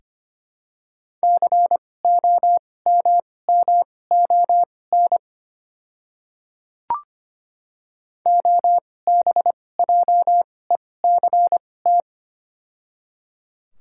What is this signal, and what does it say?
Morse code, heard as a keyed tone, about 25 wpm, spaced as for 18 wpm, 700 Hz. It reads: COMMON OBJECT
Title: 2x Character Spacing - Top 500 Words - 25wpm